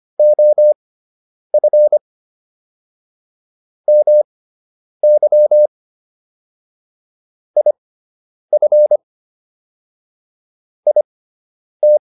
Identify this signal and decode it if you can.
Morse code, heard as a keyed tone, about 25 wpm, spaced as for 9 wpm, 600 Hz. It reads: OF MY IF IT